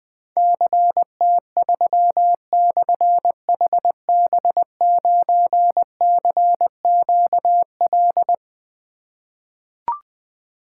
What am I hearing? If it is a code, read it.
Morse code, heard as a keyed tone, about 20 wpm, 700 Hz. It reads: CT3/HB9CQL
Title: Difficult Call Signs 20wpm